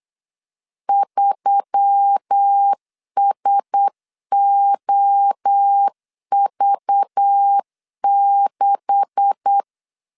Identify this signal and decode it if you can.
Morse code, heard as a keyed tone, about 8 wpm, 800 Hz. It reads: 3SOV6
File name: 8wpm